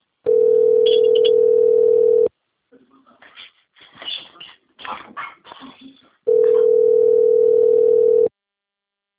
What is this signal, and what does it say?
Ringback tone (North America), 440 + 480 Hz, with speech.
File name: rg-1600-unknown- - - .89160